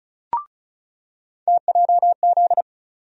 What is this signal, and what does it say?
Morse code, heard as a keyed tone, about 34 wpm, 700 Hz. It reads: TJZ
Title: USA Call Sign Postfixes - 3 Letters 35wpm